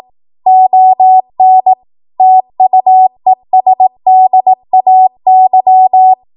Morse code, heard as a keyed tone, about 18 wpm, 750 Hz. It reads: ON TUESDAY
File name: Archive / 18 WPM / 18WPM